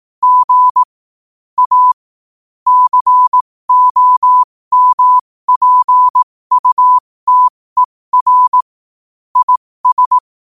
Morse code, heard as a keyed tone, about 18 wpm, spaced as for 15 wpm, 1000 Hz. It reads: G A COMPUTER IS
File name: 15wpm